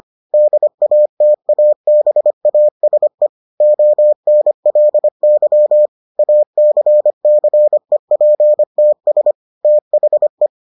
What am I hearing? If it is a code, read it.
Morse code, heard as a keyed tone, about 25 wpm, 600 Hz. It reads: DATABASE ONLY ACCEPTS THE